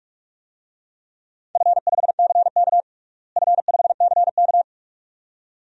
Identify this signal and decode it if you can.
Morse code, heard as a keyed tone, about 42 wpm, 700 Hz. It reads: F5CK F5CK
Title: DXCC-100 Call Signs - 4 Characters - Repeated 2x - 45wpm